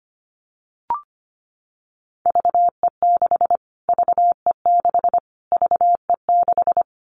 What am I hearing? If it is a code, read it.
Morse code, heard as a keyed tone, about 25 wpm, 700 Hz. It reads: VE6 VE6 VE6
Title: Callsign Prefixes for Top 25 DXCC - Repeated 3x - 25wpm